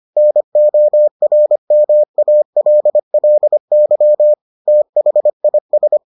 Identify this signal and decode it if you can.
Morse code, heard as a keyed tone, about 25 wpm, 600 Hz. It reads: NORMALLY THIS